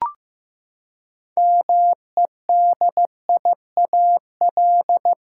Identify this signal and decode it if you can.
Morse code, heard as a keyed tone, about 15 wpm, 700 Hz. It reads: MEDIAL
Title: Practice QSO - Exchange 3 - More Info - 15wpm